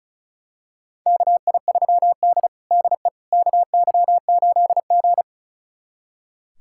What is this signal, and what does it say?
Morse code, heard as a keyed tone, about 36 wpm, 700 Hz. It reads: KI3D DE KY8G
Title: USA Callsign DE Callsign - 4 Characters - 35wpm